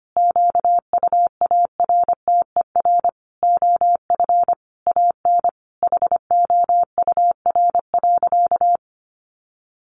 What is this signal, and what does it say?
Morse code, heard as a keyed tone, about 25 wpm, 700 Hz. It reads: QUARTER OF AN HOUR.
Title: Sentences from Top 1300 Words 25wpm